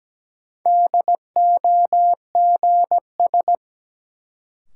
Morse code, heard as a keyed tone, about 17 wpm, 700 Hz. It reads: DOGS
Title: Sentences from Top 1600 Words - New Word Drill Alt - 17wpm